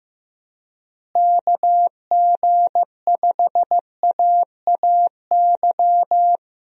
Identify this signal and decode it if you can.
Morse code, heard as a keyed tone, about 15 wpm, 700 Hz. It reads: KG5AAY
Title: USA Call Signs - 6 Characters - Repeated 3x - 15wpm